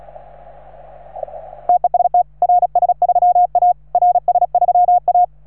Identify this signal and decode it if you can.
Morse code, heard as a keyed tone, about 35 wpm, 710 Hz. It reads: TEST RS3A RS3A
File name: RS3A-20